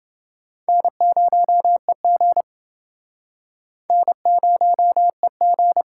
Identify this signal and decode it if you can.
Morse code, heard as a keyed tone, about 28 wpm, 700 Hz. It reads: N0EG N0EG